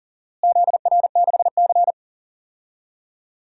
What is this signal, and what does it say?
Morse code, heard as a keyed tone, about 37 wpm, 700 Hz. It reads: ZR6C
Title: DXCC-100 Call Signs - 4 Characters - Repeated 3x - 40wpm